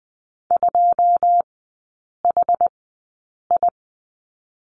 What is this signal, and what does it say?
Morse code, heard as a keyed tone, about 20 wpm, spaced as for 8 wpm, 700 Hz. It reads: 2HI
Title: 093 Lesson 16e - Letter c - Three Characters 20wpm